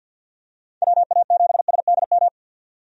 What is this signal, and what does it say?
Morse code, heard as a keyed tone, about 47 wpm, 700 Hz. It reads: WA7SDM
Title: USA Call Signs - 6 Characters - 50wpm